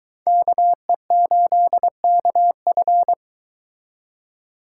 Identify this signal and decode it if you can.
Morse code, heard as a keyed tone, about 23 wpm, 700 Hz. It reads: KE8KF